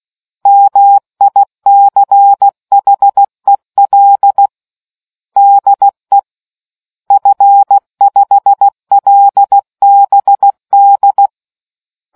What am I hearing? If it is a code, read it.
Morse code, heard as a keyed tone, about 16 wpm, 800 Hz. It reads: MICHEL DE F5LBD